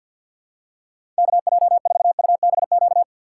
Morse code, heard as a keyed tone, about 48 wpm, 700 Hz. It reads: KJ4UBQ
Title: USA Call Signs - 6 Characters - 3x Character Spacing - 50wpm